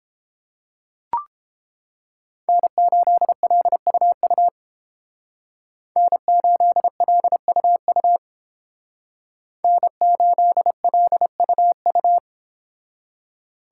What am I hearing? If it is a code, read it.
Morse code, heard as a keyed tone, about 30 wpm, 700 Hz. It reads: N8LUU N8LUU N8LUU